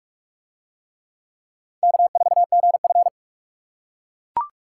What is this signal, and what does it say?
Morse code, heard as a keyed tone, about 44 wpm, 700 Hz. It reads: K4GF